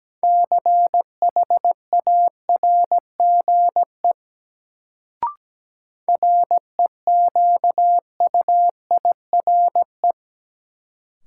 Morse code, heard as a keyed tone, about 17 wpm, 700 Hz. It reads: CHARGE REQUIRE
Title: Sentences from Top 900 Words - New Word Drill - 17wpm